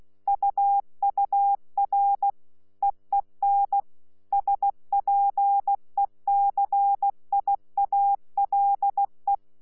Morse code, heard as a keyed tone, about 16 wpm, 800 Hz. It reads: UUR EEN SPECIALE
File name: Z PI4VRZ / PI4VRZ 14-jun-2025